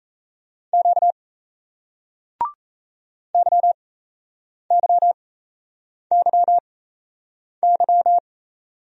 Morse code, heard as a keyed tone, about 37 wpm, spaced as for 8 wpm, 700 Hz. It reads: Q YYYY